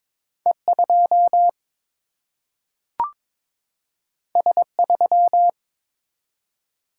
Morse code, heard as a keyed tone, about 22 wpm, 700 Hz. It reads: E 2 S 3
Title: Single-Letter Single-Number 22wpm